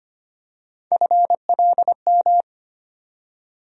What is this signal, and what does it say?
Morse code, heard as a keyed tone, about 25 wpm, 700 Hz. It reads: FLM
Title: USA Call Sign Postfixes - 3 Letters 25wpm